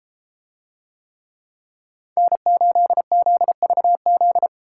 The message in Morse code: N8ZVZ

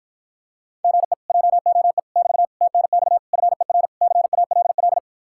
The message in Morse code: GE JOE <BT> TNX FER CALL

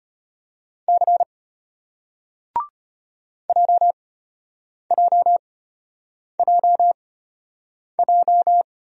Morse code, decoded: C JJJJ

